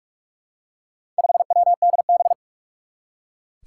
Morse code, heard as a keyed tone, about 44 wpm, 700 Hz. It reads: 5WDB
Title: Four Challenging Letter-Number Combinations 45wpm